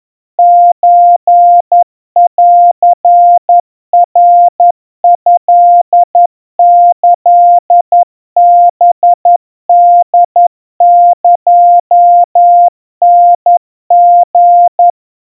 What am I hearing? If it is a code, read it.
Morse code (Japanese Wabun code), heard as a keyed tone, about 11 wpm, 700 Hz. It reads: ソンナトキハホエタリ
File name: kawa / Morse sample / kamyu 40 cpm 700 Hz Crispy